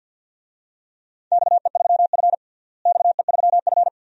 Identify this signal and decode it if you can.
Morse code, heard as a keyed tone, about 52 wpm, 700 Hz. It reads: XE3F XE3F
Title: DXCC-100 Call Signs - 4 Characters - Repeated 2x - 50wpm